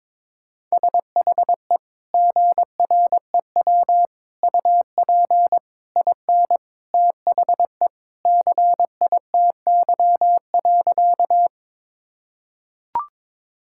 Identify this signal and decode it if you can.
Morse code, heard as a keyed tone, about 21 wpm, 700 Hz. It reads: SHE GREW UP IN THE CITY.
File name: practice / 22wpm / Sentences with 2-Word Phrases - 22wpm